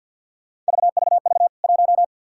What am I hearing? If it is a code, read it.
Morse code, heard as a keyed tone, about 52 wpm, 700 Hz. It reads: VVV 1
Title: K1USN SST QSO Practice 50wpm